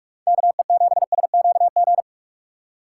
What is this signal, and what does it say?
Morse code, heard as a keyed tone, about 41 wpm, 700 Hz. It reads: KE7SQG
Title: USA Call Signs - 6 Characters - 6x Character Spacing - 45wpm